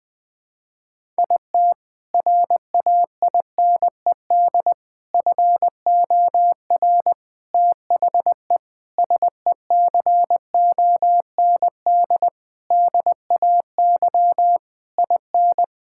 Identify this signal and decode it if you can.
Morse code, heard as a keyed tone, about 20 wpm, 700 Hz. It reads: IT RAINED FOR THE SECOND DAY IN
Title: Sentences with 7-Word Phrases - 20wpm